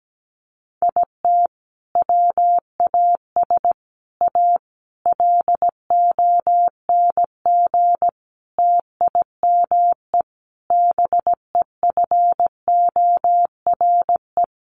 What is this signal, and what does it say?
Morse code, heard as a keyed tone, about 17 wpm, 700 Hz. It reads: IT WAS A LONG TIME BEFORE
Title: Sentences from Top 500 Words 17wpm